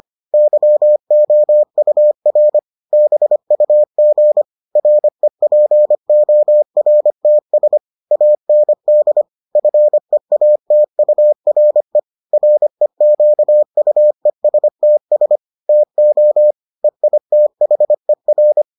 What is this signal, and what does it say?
Morse code, heard as a keyed tone, about 25 wpm, 600 Hz. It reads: YOUR BUG REPORTS AND FEATURE REQUESTS TO EITHER